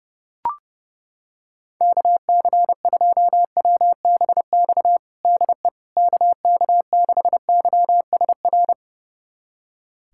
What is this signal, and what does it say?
Morse code, heard as a keyed tone, about 32 wpm, 700 Hz. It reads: KC2WBX DE KK6YSR
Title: USA Callsign DE Callsign - 6 Characters - 30wpm